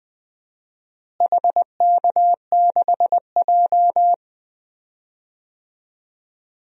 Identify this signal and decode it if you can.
Morse code, heard as a keyed tone, about 20 wpm, 700 Hz. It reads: HK6J
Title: DXCC-100 Call Signs - 4 Characters - Repeated 3x - 20wpm